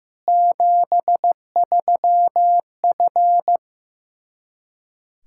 Morse code, heard as a keyed tone, about 16 wpm, 700 Hz. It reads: 73F
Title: Practice QSO - Exchange 2 - Info - 15wpm